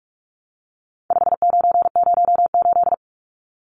Morse code, heard as a keyed tone, about 44 wpm, 700 Hz. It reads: 5908
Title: Numbers - 4-Digit 45wpm